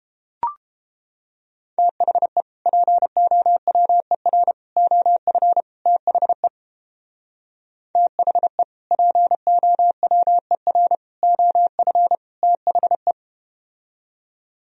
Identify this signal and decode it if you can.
Morse code, heard as a keyed tone, about 30 wpm, 700 Hz. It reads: THE POWER OF THE THE POWER OF THE